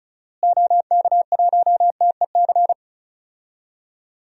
Morse code, heard as a keyed tone, about 36 wpm, 700 Hz. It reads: OK1TEC
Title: DXCC-100 Call Signs - 6 Characters - Repeated 3x - 35wpm